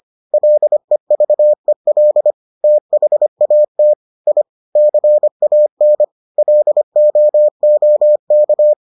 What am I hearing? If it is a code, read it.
Morse code, heard as a keyed tone, about 25 wpm, 600 Hz. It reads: LEVEL THAT I CAN LOOK